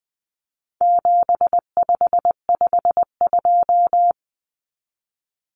Morse code, heard as a keyed tone, about 20 wpm, 700 Hz. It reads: 7552
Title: Common Cut Numbers - 4-Digit 20wpm